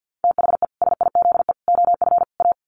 Morse code, heard as a keyed tone, about 50 wpm, 700 Hz. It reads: THE SIZE OF A
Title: Sets of 4 Words 50wpm